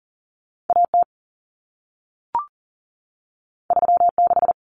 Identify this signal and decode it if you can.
Morse code, heard as a keyed tone, about 40 wpm, 700 Hz. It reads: AT 36
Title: Common Cut Numbers - 2-Digit 40wpm